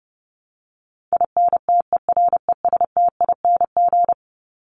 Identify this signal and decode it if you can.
Morse code, heard as a keyed tone, about 30 wpm, 700 Hz. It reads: INTERESTING
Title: ING Suffix - Top 100 Words with Base Words 30wpm